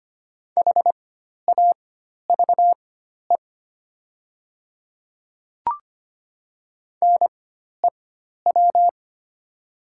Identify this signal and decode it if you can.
Morse code, heard as a keyed tone, about 25 wpm, spaced as for 12 wpm, 690 Hz. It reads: HAVE NEW
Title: 4x Character Spacing - Top 200 Words - 25wpm